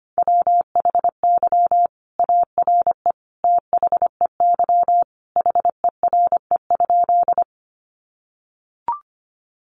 Morse code, heard as a keyed tone, about 25 wpm, 700 Hz. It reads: WHY ARE THEY HERE?
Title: Sentences from Top 200 Words 25wpm